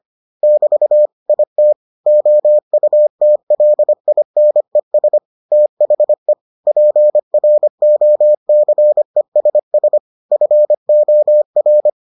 Morse code, heard as a keyed tone, about 25 wpm, 600 Hz. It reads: <BT> IT OUTLINES THE PROCESS FOR